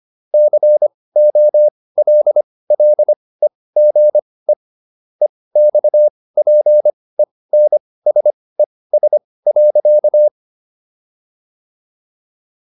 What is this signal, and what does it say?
Morse code, heard as a keyed tone, about 25 wpm, spaced as for 18 wpm, 600 Hz. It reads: COLLEGE EXPENSES.